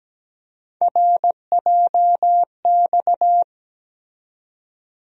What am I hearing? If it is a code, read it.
Morse code, heard as a keyed tone, about 17 wpm, 700 Hz. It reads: RJX
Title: USA Call Sign Postfixes - 3 Letters 17wpm